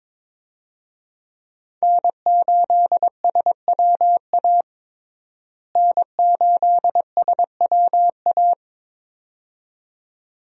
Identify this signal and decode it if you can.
Morse code, heard as a keyed tone, about 22 wpm, 700 Hz. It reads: N8SWA N8SWA
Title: USA Call Signs - 5 Characters - Repeated 2x - 22wpm